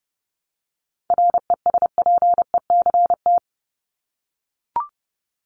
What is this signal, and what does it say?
Morse code, heard as a keyed tone, about 30 wpm, 700 Hz. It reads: RESPECT
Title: Top 1501-1750 Words 30wpm